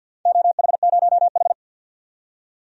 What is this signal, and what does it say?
Morse code, heard as a keyed tone, about 50 wpm, 700 Hz. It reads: OH0H
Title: DXCC-100 Call Signs - 4 Characters - 50wpm